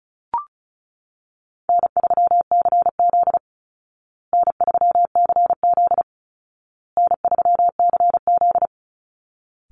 Morse code, heard as a keyed tone, about 34 wpm, 700 Hz. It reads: N3CZ N3CZ N3CZ
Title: Easy Call Signs Repeated 3x - 35wpm